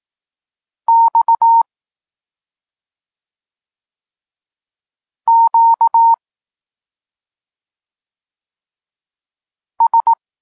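Morse code, heard as a keyed tone, about 18 wpm, 920 Hz. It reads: X Q S
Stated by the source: Polish Morse lesson teaching the letter C, from SP9MOA Niepołomicki Klub Krótkofalowców.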